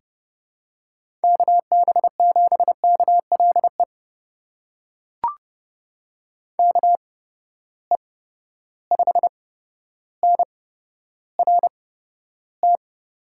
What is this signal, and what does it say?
Morse code, heard as a keyed tone, about 29 wpm, spaced as for 13 wpm, 700 Hz. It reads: KB7KLE KE5NRT
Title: USA Call Signs - 6 Characters - 8x Character Spacing - 30wpm